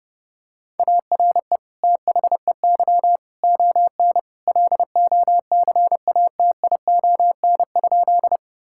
Morse code, heard as a keyed tone, about 30 wpm, 700 Hz. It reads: ARE THEY ON LOCATION?
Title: Sentences from Top 1000 Words - Review Part 6 - 30wpm